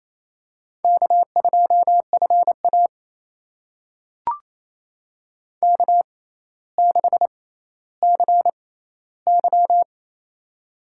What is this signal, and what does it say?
Morse code, heard as a keyed tone, about 28 wpm, spaced as for 14 wpm, 700 Hz. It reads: K2FA K6CY
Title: USA Call Signs - 4 Characters - 6x Character Spacing - 28wpm